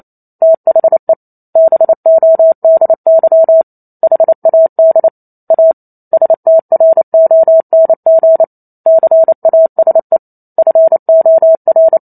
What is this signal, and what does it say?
Morse code, heard as a keyed tone, about 28 wpm, 650 Hz. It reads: THE BODY HAD A STRONG CASE FOR
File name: Chapter 2 - 28 WPM